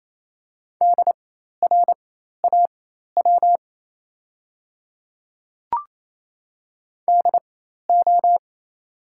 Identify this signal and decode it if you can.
Morse code, heard as a keyed tone, about 28 wpm, spaced as for 13 wpm, 690 Hz. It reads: DRAW DO
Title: 4x Character Spacing - Top 300 Words - 28wpm